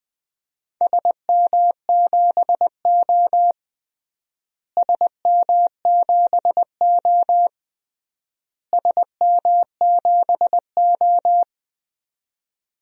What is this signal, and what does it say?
Morse code, heard as a keyed tone, about 20 wpm, 700 Hz. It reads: SM7O SM7O SM7O
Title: DXCC-100 Call Signs - 4 Characters - Repeated 3x - 20wpm